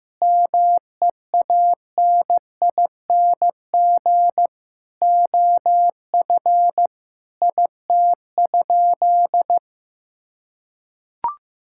Morse code, heard as a keyed tone, about 15 wpm, 700 Hz. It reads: MEANING OF IT?
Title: Sentences from Top 1000 Words - Review Part 6 - 15wpm